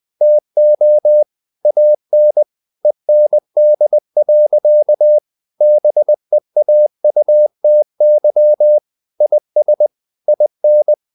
Morse code, heard as a keyed tone, about 20 wpm, 600 Hz. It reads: TO AN END. BEAUTY IS IN
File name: CW / Practice / EnglishSayings20